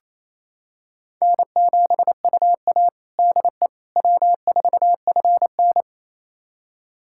Morse code, heard as a keyed tone, about 28 wpm, 700 Hz. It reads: N7UA DE W4FN
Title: USA Callsign DE Callsign - 4 Characters - 28wpm